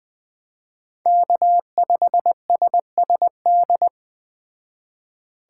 Morse code, heard as a keyed tone, about 20 wpm, 700 Hz. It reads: K5SSD
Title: Speed Racing - USA 1x3 Call Signs 17wpm